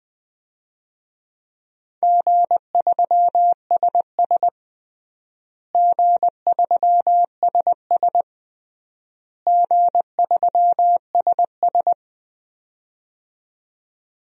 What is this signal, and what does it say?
Morse code, heard as a keyed tone, about 20 wpm, 700 Hz. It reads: G3SS G3SS G3SS
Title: DXCC-100 Call Signs - 4 Characters - Repeated 3x - 20wpm